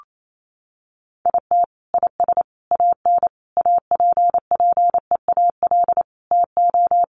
Morse code, heard as a keyed tone, about 28 wpm, 700 Hz. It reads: IT IS AN APPEAL TO
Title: Sentences from Top 1300 Words 28wpm